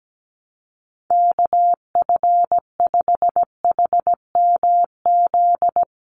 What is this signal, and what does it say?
Morse code, heard as a keyed tone, about 17 wpm, 700 Hz. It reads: KF5HMZ
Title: USA Call Signs 17wpm